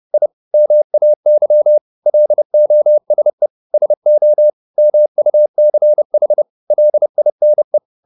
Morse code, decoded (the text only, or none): I MAY LOSE SO MUCH LINE